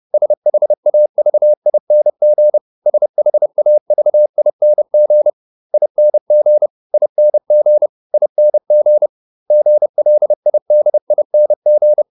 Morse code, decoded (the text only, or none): SHAVING SHAVING ING ING ING GLIDING